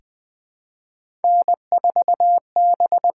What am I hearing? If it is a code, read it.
Morse code, heard as a keyed tone, about 20 wpm, 700 Hz. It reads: N4B
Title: USA Call Signs - 4 Characters - 6x Character Spacing - 20wpm